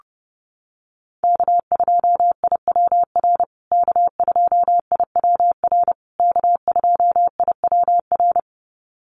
Morse code, heard as a keyed tone, about 30 wpm, 700 Hz. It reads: K2IWR K2IWR K2IWR
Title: USA Call Signs Repeated 3x 30wpm